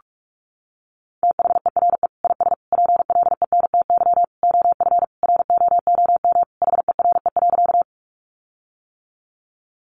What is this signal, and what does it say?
Morse code, heard as a keyed tone, about 44 wpm, 700 Hz. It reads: THERE IS PLENTY OF ROOM HERE.
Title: Sentences from Top 1300 Words 45wpm